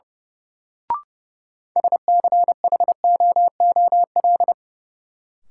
Morse code, heard as a keyed tone, about 31 wpm, 700 Hz. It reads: SCHOOL
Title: Sets of 2 Words - New Word Drill - 30wpm